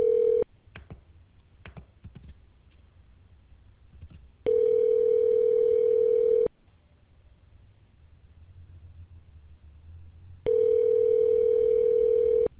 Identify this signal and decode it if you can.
Ringback tone (North America), 440 + 480 Hz.